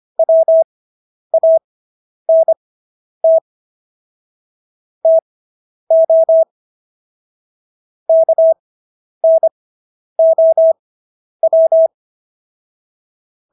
Morse code, heard as a keyed tone, about 25 wpm, spaced as for 10 wpm, 650 Hz. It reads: WANT TO KNOW